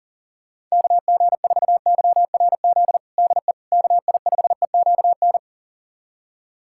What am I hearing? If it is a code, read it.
Morse code, heard as a keyed tone, about 40 wpm, 690 Hz. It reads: KG4YRZ DE KI5EQN
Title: USA Callsign DE Callsign - 6 Characters - 40wpm